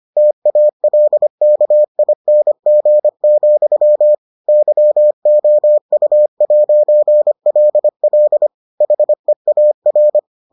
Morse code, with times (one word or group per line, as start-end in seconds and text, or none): 0.2-4.1 TALKING,
4.5-8.5 YOU'LL
8.8-10.2 HEAR